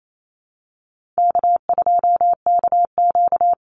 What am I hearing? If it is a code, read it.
Morse code, heard as a keyed tone, about 28 wpm, 700 Hz. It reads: K2KQ
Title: Call Signs - 28wpm